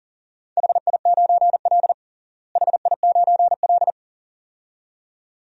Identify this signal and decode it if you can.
Morse code, heard as a keyed tone, about 40 wpm, 700 Hz. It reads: HI9L HI9L
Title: DXCC-100 Call Signs - 4 Characters - Repeated 2x - 40wpm